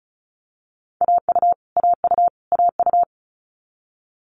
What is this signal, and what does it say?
Morse code, heard as a keyed tone, about 36 wpm, 700 Hz. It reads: AU AU AU
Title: Multi-letter Phonemes Repeated 3x 35wpm